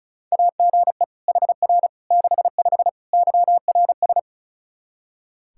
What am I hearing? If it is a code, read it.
Morse code, heard as a keyed tone, about 34 wpm, 700 Hz. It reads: AGE HR 65 YRS
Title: Practice QSO - Exchange 3 - More Info - 35wpm